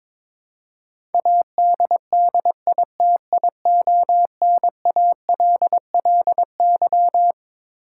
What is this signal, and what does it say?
Morse code, heard as a keyed tone, about 22 wpm, 700 Hz. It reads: ADDITIONALLY
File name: Twelve Letter Words - 22wpm